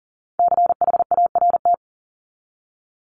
Morse code, heard as a keyed tone, about 40 wpm, 700 Hz. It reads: CHART
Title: Top 1251-1500 Words 40wpm